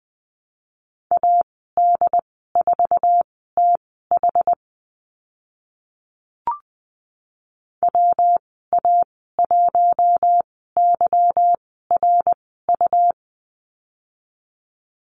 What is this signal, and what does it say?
Morse code, heard as a keyed tone, about 20 wpm, spaced as for 15 wpm, 700 Hz. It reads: AD4TH WA1YRU
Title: 161 Lesson 26g - Letter g - Call Signs 20wpm